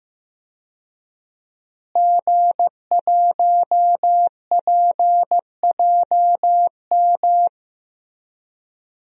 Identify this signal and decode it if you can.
Morse code, heard as a keyed tone, about 15 wpm, 700 Hz. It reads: G1PJM
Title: DXCC-100 Call Signs - 5 Characters - Repeated 3x - 15wpm